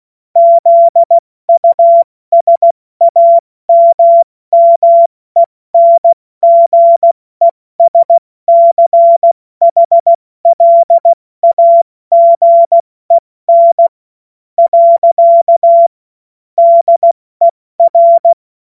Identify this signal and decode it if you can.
Morse code, heard as a keyed tone, about 16 wpm, spaced as for 14 wpm, 680 Hz. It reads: ZUSAMMENGESCHLAGEN. DER
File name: Morsekurs / Witze / Witz 26. April 2022 70 F80